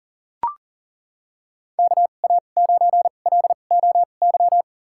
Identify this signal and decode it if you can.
Morse code, heard as a keyed tone, about 40 wpm, spaced as for 29 wpm, 700 Hz. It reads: KA9LOY
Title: USA Call Signs - 6 Characters - 2x Character Spacing - 40wpm